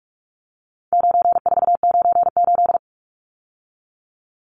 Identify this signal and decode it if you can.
Morse code, heard as a keyed tone, about 44 wpm, 700 Hz. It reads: 9498
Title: Numbers - 4-Digit 45wpm